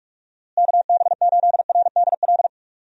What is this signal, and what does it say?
Morse code, heard as a keyed tone, about 46 wpm, 700 Hz. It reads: KB8RDL
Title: USA Call Signs - 6 Characters - Repeated 2x - 45wpm